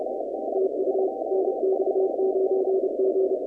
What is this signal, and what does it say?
Morse code, heard as a keyed tone, about 29 wpm, 380 Hz, with noise. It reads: TU NX9G
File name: contests / 2019-cw-wpx-kt5j-k5pi- -2330 0-20